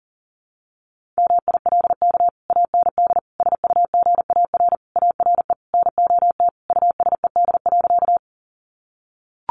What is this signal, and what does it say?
Morse code, heard as a keyed tone, about 40 wpm, 700 Hz. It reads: MILK AND SUGAR ARE NOT USED.
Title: Sentences from Top 1500 Words 40wpm